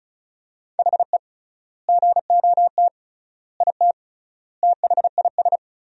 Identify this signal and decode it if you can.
Morse code, heard as a keyed tone, about 34 wpm, 700 Hz. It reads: HE GOT IT THIS
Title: 3x Word-Spacing - Sentences from Top 400 Words - 35wpm